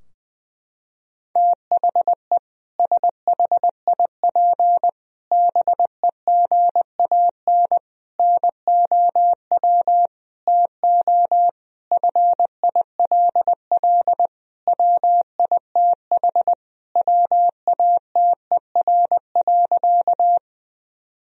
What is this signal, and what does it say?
Morse code, heard as a keyed tone, about 21 wpm, 700 Hz. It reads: THE SHIP BEGAN NOW TO FILL WITH WATER.
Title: Sentences from Top 1000 Words - Review Part 3 - 20wpm